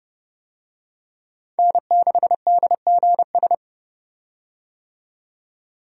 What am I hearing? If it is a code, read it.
Morse code, heard as a keyed tone, about 30 wpm, 700 Hz. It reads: N6DGS
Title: USA Call Signs - 5 Characters - 30wpm